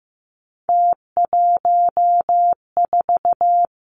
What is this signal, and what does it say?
Morse code, heard as a keyed tone, about 15 wpm, 700 Hz. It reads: T14
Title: Three Challenging Letter-Number Combinations 15wpm